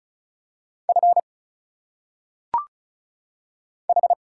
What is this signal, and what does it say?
Morse code, heard as a keyed tone, about 34 wpm, 700 Hz. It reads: F H
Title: Single Letters - Phonetic Pronunciation 35wpm